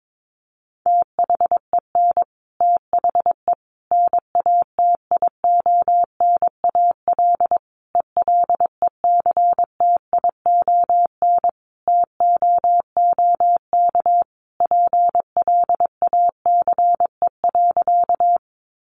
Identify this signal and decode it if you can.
Morse code, heard as a keyed tone, about 22 wpm, 700 Hz. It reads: THEN THE NATIONAL ELECTION TOOK PLACE.
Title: Sentences from Top 1500 Words 22wpm